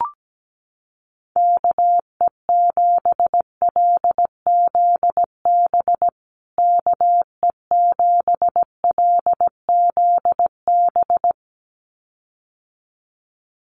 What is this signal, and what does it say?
Morse code, heard as a keyed tone, about 17 wpm, 700 Hz. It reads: KE7LZB KE7LZB
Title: USA Call Signs Repeated 2x 17wpm